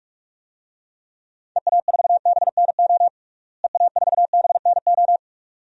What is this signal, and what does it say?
Morse code, heard as a keyed tone, about 46 wpm, 700 Hz. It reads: EA4BNO EA4BNO
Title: DXCC-100 Call Signs - 6 Characters - Repeated 2x - 45wpm